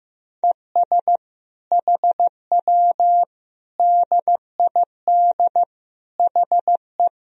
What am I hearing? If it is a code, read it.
Morse code, heard as a keyed tone, about 15 wpm, 700 Hz. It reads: ES HW DID HE
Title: Sentences from Top 400 Words and Abbreviations 15wpm